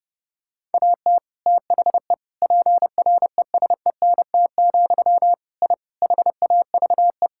Morse code, heard as a keyed tone, about 30 wpm, 700 Hz. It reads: AT THE PRESENT, I HAVE
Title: Sentences from Top 1000 Words - Review Part 5 - 30wpm